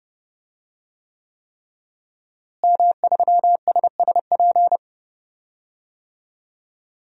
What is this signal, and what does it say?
Morse code, heard as a keyed tone, about 32 wpm, 700 Hz. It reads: M3SSP
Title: DXCC-100 Call Signs - 5 Characters - Repeated 3x - 30wpm